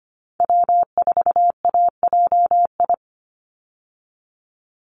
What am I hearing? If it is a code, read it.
Morse code, heard as a keyed tone, about 25 wpm, 700 Hz. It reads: W4AJI
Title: USA Call Signs 25wpm